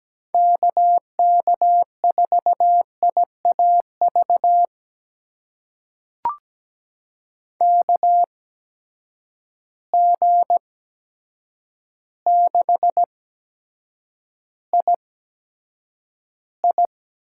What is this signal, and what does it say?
Morse code, heard as a keyed tone, about 17 wpm, spaced as for 8 wpm, 700 Hz. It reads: KK4IAV KG6II
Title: USA Call Signs - 6 Characters - 8x Character Spacing - 17wpm